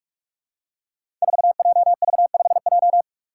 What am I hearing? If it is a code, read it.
Morse code, heard as a keyed tone, about 43 wpm, 700 Hz. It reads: 4JV5J
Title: Five Challenging Letter-Number Combinations 45wpm